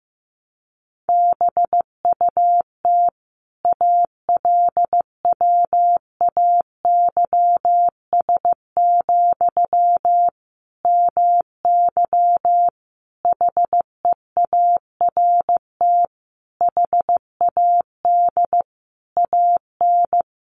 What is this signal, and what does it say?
Morse code, heard as a keyed tone, about 15 wpm, 700 Hz. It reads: BUT ALWAYS, MY HEART HAD AN
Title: Sentences from Top 1200 Words 15wpm